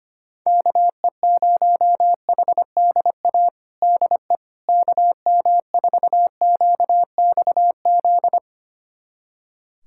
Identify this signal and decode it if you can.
Morse code, heard as a keyed tone, about 25 wpm, 700 Hz. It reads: KE0HDA DE KM4QXZ
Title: USA Callsign DE Callsign - 6 Characters - 25wpm